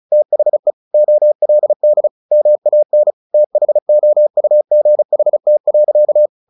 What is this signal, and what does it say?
Morse code, heard as a keyed tone, about 36 wpm, 600 Hz. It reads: THE OLD MAN THOUGHT.
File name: the old man and the sea / the old man and the sea 1 35wpm